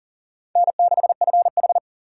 Morse code, heard as a keyed tone, about 37 wpm, 700 Hz. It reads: N6FH
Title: USA Call Signs - 4 Characters - 6x Character Spacing - 40wpm